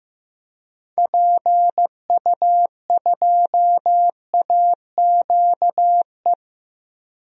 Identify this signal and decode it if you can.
Morse code, heard as a keyed tone, about 16 wpm, 700 Hz. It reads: PU2AQE
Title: DXCC-100 Call Signs - 6 Characters - Repeated 2x - 15wpm